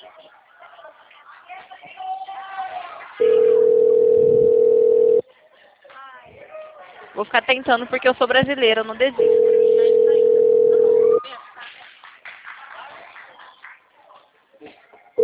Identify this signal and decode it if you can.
Ringback tone (North America), 440 + 480 Hz, with speech.
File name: out- -2016- - - .16